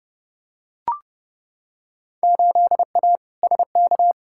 Morse code, heard as a keyed tone, about 32 wpm, 700 Hz. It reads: 8A SK
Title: Field Day Exchanges Encore - 30wpm